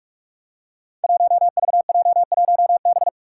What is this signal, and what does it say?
Morse code, heard as a keyed tone, about 46 wpm, 700 Hz. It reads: 1VJ1B
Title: Five Challenging Letter-Number Combinations 45wpm